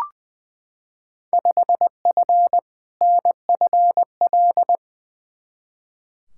Morse code, heard as a keyed tone, about 21 wpm, 700 Hz. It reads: 5F NFL
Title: Field Day Exchanges Encore - 20wpm